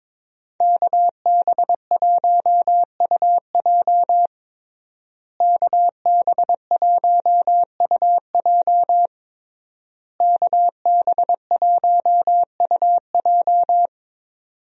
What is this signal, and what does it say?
Morse code, heard as a keyed tone, about 22 wpm, 700 Hz. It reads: KB1UJ KB1UJ KB1UJ
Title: USA Call Signs - 5 Characters - Repeated 3x - 22wpm